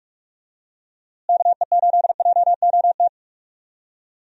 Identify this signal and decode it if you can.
Morse code, heard as a keyed tone, about 47 wpm, 700 Hz. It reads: KE8JOT
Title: USA Call Signs - 6 Characters - Repeated 3x - 45wpm